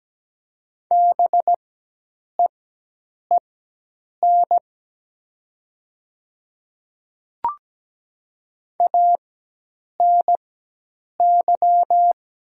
Morse code, heard as a keyed tone, about 17 wpm, spaced as for 8 wpm, 700 Hz. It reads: BEEN ANY
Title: 4x Character Spacing - Top 500 Words - 17wpm